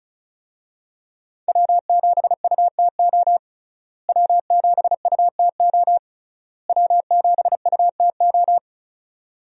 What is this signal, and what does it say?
Morse code, heard as a keyed tone, about 36 wpm, 700 Hz. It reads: W7UTO W7UTO W7UTO
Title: USA Call Signs - 5 Characters - Repeated 3x - 35wpm